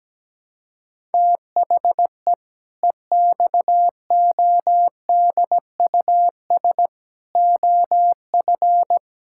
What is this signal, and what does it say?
Morse code, heard as a keyed tone, about 17 wpm, 700 Hz. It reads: THE EXODUS OF
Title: Pangram Challenge v2 - 17wpm